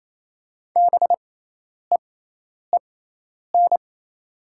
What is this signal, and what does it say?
Morse code, heard as a keyed tone, about 28 wpm, spaced as for 10 wpm, 700 Hz. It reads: BEEN
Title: 6x Character Spacing - Top 200 Words - 28wpm